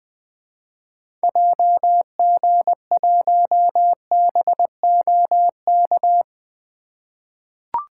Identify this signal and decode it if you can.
Morse code, heard as a keyed tone, about 20 wpm, 700 Hz. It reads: JG1BOK
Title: 2023 SOTA Activator Call Signs - 20wpm